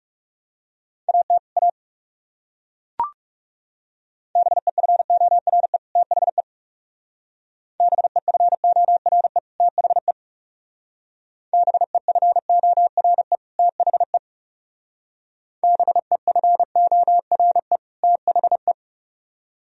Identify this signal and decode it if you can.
Morse code, heard as a keyed tone, about 36 wpm, 700 Hz. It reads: AT A BEFORE THE BEFORE THE BEFORE THE BEFORE THE
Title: Speed-Racing - Sets of 2 Words 30wpm